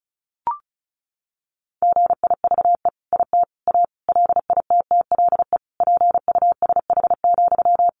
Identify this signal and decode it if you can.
Morse code, heard as a keyed tone, about 34 wpm, 700 Hz. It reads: GIVE IT A LITTLE PUSH,
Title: Sentences from Top 1500 Words 35wpm